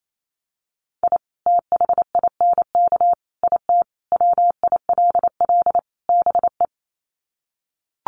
Morse code, heard as a keyed tone, about 28 wpm, 700 Hz. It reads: I THINK IT WILL BE
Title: Sets of 5 Words Encore2 28wpm